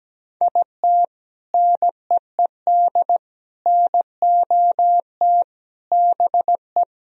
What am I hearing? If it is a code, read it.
Morse code, heard as a keyed tone, about 17 wpm, 700 Hz. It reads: IT NEED NOT BE
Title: Sentences from Top 1000 Words - Review Part 4 - 17wpm